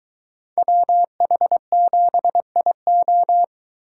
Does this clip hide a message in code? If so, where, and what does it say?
Yes: Morse code from 0.6 to 3.4 s, heard as a keyed tone, about 22 wpm, 700 Hz. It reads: WH7IO